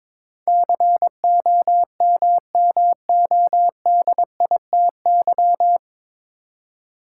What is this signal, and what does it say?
Morse code, heard as a keyed tone, about 22 wpm, 700 Hz. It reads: COMMODITY
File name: 9 Letter Words - First 3 Letters Given - 22wpm